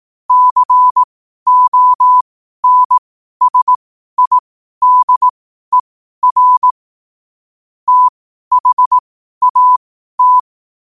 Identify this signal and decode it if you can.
Morse code, heard as a keyed tone, about 18 wpm, spaced as for 12 wpm, 1000 Hz. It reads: CONSIDER THAT
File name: qotdfiles / 12wpm